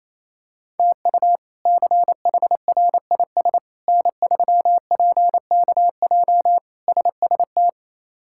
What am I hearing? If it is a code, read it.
Morse code, heard as a keyed tone, about 27 wpm, 700 Hz. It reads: TU CHRIS N3PKJ SST
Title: K1USN SST QSO Practice - 28wpm